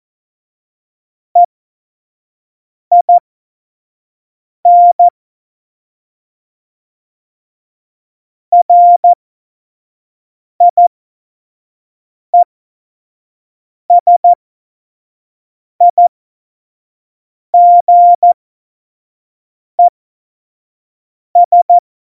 Morse code, heard as a keyed tone, about 14 wpm, spaced as for 5 wpm, 700 Hz. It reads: EIN RIESIGES
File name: Morsekurs / Witze / Witz 17.Juli 2018, 25 F70